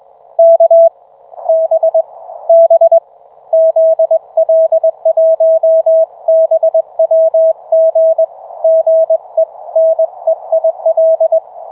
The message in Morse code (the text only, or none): K B B ZL1BWG GE NEIL